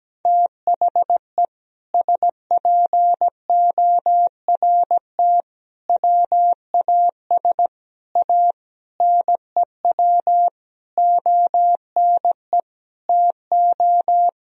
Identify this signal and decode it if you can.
Morse code, heard as a keyed tone, about 17 wpm, 700 Hz. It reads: THE SPORT WAS A NEW ONE TO
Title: Sentences from Top 1000 Words - Review Part 4 - 17wpm